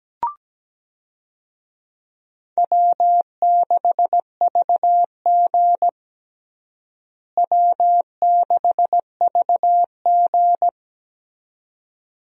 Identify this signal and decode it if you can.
Morse code, heard as a keyed tone, about 17 wpm, 700 Hz. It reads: W6VG W6VG
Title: USA Call Signs - 4 Characters - Repeated 2x - 17wpm